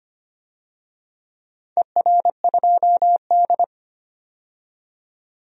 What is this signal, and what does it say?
Morse code, heard as a keyed tone, about 25 wpm, 700 Hz. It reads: ER2D
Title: DXCC-100 Call Signs - 4 Characters - 25wpm